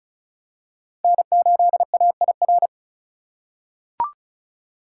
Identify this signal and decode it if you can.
Morse code, heard as a keyed tone, about 36 wpm, 700 Hz. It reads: N8AIR
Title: USA Call Signs - 5 Characters - 4x Character Spacing - 35wpm